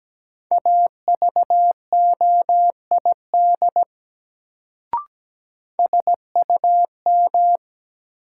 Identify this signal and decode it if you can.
Morse code, heard as a keyed tone, about 17 wpm, 700 Hz. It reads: AVOID SUM
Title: Sentences from Top 1000 Words - Review Part 6 - New Word Drill - 17wpm